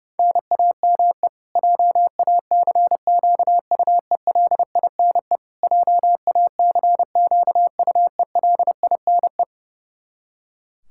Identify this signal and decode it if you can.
Morse code, heard as a keyed tone, about 32 wpm, 700 Hz. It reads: NAME JACQUELINE JACQUELINE
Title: K1USN SST QSO Practice 30wpm